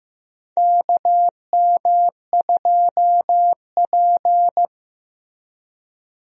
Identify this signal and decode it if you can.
Morse code, heard as a keyed tone, about 16 wpm, 700 Hz. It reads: KM2P
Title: USA Call Signs - 4 Characters - Repeated 3x - 15wpm